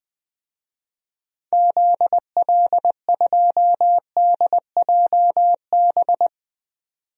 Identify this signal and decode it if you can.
Morse code, heard as a keyed tone, about 20 wpm, 700 Hz. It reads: ZL2DJB
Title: DXCC-100 Call Signs - 6 Characters - 20wpm